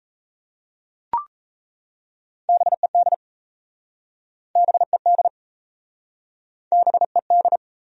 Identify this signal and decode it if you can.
Morse code, heard as a keyed tone, about 39 wpm, 690 Hz. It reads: BED BED BED